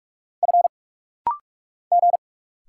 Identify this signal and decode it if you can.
Morse code, heard as a keyed tone, about 46 wpm, 700 Hz. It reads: F G
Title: Single Letter-Number - Rapid-Fire 45wpm